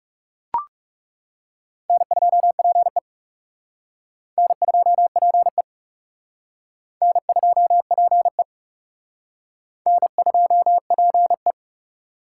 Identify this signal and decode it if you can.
Morse code, heard as a keyed tone, about 36 wpm, 700 Hz. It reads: N2PE N2PE N2PE N2PE
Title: Speed Racing - USA 1x2 Call Signs 30wpm